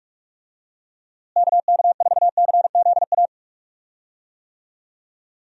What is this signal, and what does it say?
Morse code, heard as a keyed tone, about 44 wpm, 700 Hz. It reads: KK4CZA
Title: USA Call Signs - 6 Characters - Repeated 3x - 45wpm